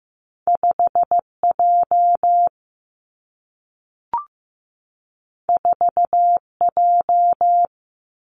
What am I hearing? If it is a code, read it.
Morse code, heard as a keyed tone, about 15 wpm, 700 Hz. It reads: 5 J 4 J
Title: Two Challenging Letter-Number Combinations 15wpm